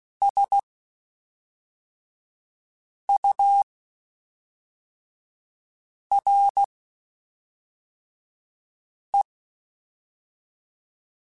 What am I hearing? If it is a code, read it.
Morse code, heard as a keyed tone, about 16 wpm, spaced as for 3 wpm, 800 Hz. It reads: SURE
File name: Spaced / Chuckles Spaced-mp3s / BrainMouth sp @5